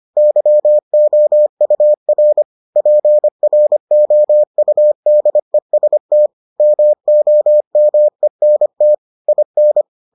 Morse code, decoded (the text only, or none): YOUR PROUDEST MOMENT IN